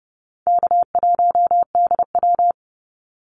Morse code, heard as a keyed tone, about 30 wpm, 700 Hz. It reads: K1DW
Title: Easy Call Signs - 30wpm